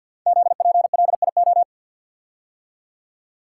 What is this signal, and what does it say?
Morse code, heard as a keyed tone, about 52 wpm, 700 Hz. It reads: ZPLIO
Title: Code Groups - 5 Letters - 50wpm